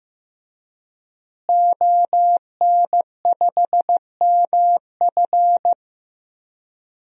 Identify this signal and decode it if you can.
Morse code, heard as a keyed tone, about 15 wpm, 700 Hz. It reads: ON5MF
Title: DXCC-100 Call Signs - 5 Characters - Repeated 3x - 15wpm